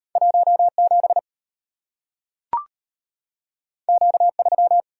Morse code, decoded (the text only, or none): 17 Q3